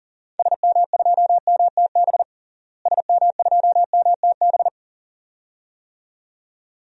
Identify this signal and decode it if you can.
Morse code, heard as a keyed tone, about 40 wpm, 700 Hz. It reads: SM2MTB SM2MTB
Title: DXCC-100 Call Signs - 6 Characters - Repeated 2x - 40wpm